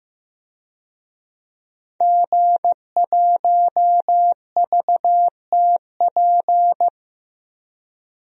Morse code, heard as a keyed tone, about 15 wpm, 700 Hz. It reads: G1VTP
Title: DXCC-100 Call Signs - 5 Characters - 15wpm